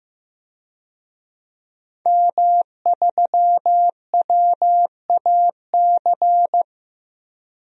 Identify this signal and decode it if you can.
Morse code, heard as a keyed tone, about 15 wpm, 700 Hz. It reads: M3WAC
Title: DXCC-100 Call Signs - 5 Characters - 15wpm